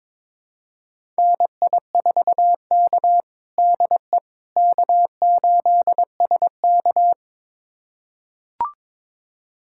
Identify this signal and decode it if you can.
Morse code, heard as a keyed tone, about 22 wpm, 700 Hz. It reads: NI4K DE K8SK
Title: USA Callsign DE Callsign - 4 Characters - 22wpm